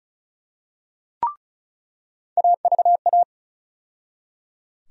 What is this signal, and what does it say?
Morse code, heard as a keyed tone, about 36 wpm, 700 Hz. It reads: AVA
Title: USA Call Sign Postfixes - 3 Letters 35wpm